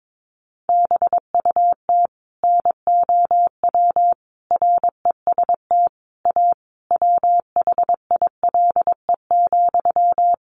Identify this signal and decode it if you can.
Morse code, heard as a keyed tone, about 22 wpm, 700 Hz. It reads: BUT NOW REST A WHILE,
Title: Sentences from Top 1100 Words 22wpm